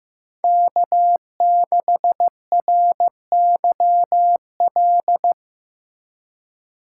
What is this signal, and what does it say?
Morse code, heard as a keyed tone, about 15 wpm, 700 Hz. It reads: K6RYL
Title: USA Call Signs - 5 Characters - 3x Character Spacing - 15wpm